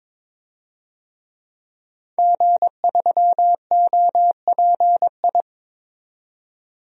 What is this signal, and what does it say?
Morse code, heard as a keyed tone, about 22 wpm, 700 Hz. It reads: G3OPI
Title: DXCC-100 Call Signs - 5 Characters - 22wpm